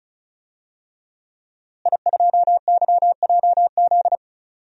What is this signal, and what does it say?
Morse code, heard as a keyed tone, about 36 wpm, 700 Hz. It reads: I2YJZ
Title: DXCC-100 Call Signs - 5 Characters - Repeated 3x - 35wpm